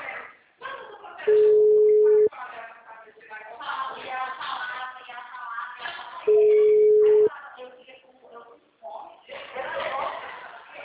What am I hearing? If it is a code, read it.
Ringback tone (Europe, 425 Hz), with speech.